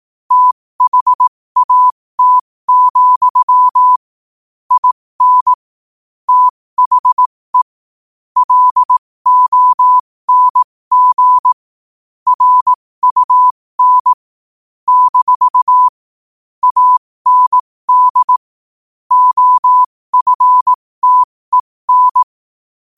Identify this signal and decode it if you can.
Morse code, heard as a keyed tone, about 18 wpm, spaced as for 15 wpm, 1000 Hz. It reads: THAT, IN THE LONG RUN - AND OFTEN
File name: qotdfiles / 15wpm